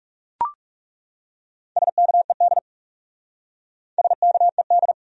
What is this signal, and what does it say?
Morse code, heard as a keyed tone, about 37 wpm, 700 Hz. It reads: SKED SKED
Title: Speed-Racing Abbreviations - 30wpm